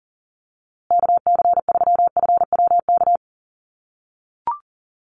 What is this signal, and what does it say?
Morse code, heard as a keyed tone, about 40 wpm, 700 Hz. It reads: KC3FWK
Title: USA Call Signs Repeated 2x 40wpm